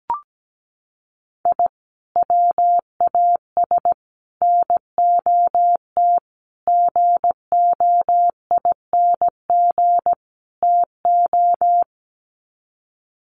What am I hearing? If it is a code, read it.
Morse code, heard as a keyed tone, about 17 wpm, 700 Hz. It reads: I WAS NOT GOING TO
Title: Sets of 5 Words 17wpm